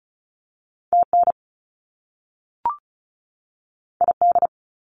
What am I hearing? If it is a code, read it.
Morse code, heard as a keyed tone, about 36 wpm, 700 Hz. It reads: TN ID
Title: US States 35wpm